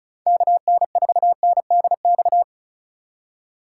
Morse code, heard as a keyed tone, about 36 wpm, 700 Hz. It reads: KN4NDX
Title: USA Call Signs - 6 Characters - 4x Character Spacing - 35wpm